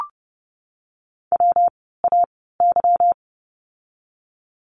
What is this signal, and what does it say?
Morse code, heard as a keyed tone, about 30 wpm, spaced as for 17 wpm, 700 Hz. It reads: WAY
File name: course / 30:10wpm / 237 Lesson 38f - Letter q - Words 30wpm